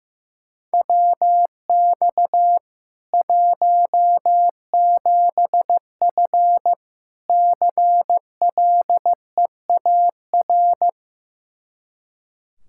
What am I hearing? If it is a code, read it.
Morse code, heard as a keyed tone, about 15 wpm, 700 Hz. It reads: WX 17F CLEAR
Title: Practice QSO - Exchange 2 - Info - 15wpm